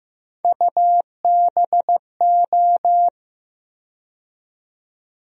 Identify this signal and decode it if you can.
Morse code, heard as a keyed tone, about 15 wpm, 700 Hz. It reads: UBO
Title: USA Call Sign Postfixes - 3 Letters 15wpm